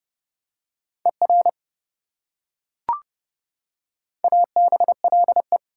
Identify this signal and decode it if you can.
Morse code, heard as a keyed tone, about 28 wpm, 700 Hz. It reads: ER ABLE